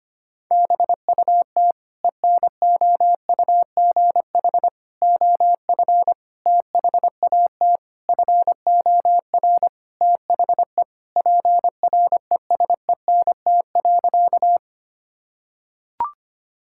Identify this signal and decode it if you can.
Morse code, heard as a keyed tone, about 25 wpm, 700 Hz. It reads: BUT ENOUGH OF THAT FOR THE PRESENT.
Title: Sentences from Top 1000 Words - Review Part 5 - 25wpm